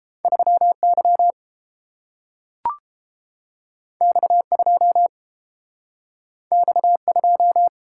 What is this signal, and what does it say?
Morse code, heard as a keyed tone, about 30 wpm, 700 Hz. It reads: 3Y X2 X2